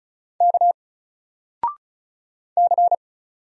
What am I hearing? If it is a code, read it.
Morse code, heard as a keyed tone, about 36 wpm, 700 Hz. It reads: K C
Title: Single Letter-Number 35wpm